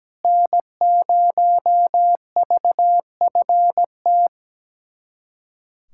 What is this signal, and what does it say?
Morse code, heard as a keyed tone, about 17 wpm, 700 Hz. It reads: N0VFT
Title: USA Callsign DE Callsign 17wpm